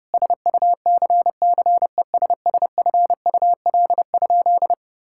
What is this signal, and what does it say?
Morse code, heard as a keyed tone, about 30 wpm, 700 Hz. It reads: SUCCESSFUL?
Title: Sentences from Top 1000 Words - Review Part 6 - 30wpm